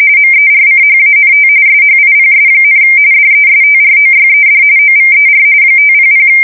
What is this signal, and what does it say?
RTTY, about 45.45 baud. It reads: MERCHANT SHIPS, DESTROYED OVER 500
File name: fsk170 / USS-New-Jersey